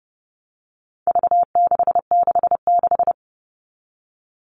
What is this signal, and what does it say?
Morse code, heard as a keyed tone, about 30 wpm, 700 Hz. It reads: V666
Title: All Cut Numbers - 4-Digit 30wpm